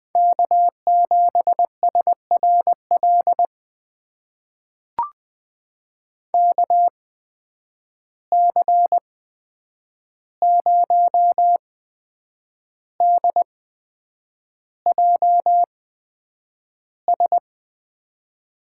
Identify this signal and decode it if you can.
Morse code, heard as a keyed tone, about 20 wpm, spaced as for 8 wpm, 700 Hz. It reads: K7SRL KC0DJS
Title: USA Call Signs - 8x Character Spacing - 20wpm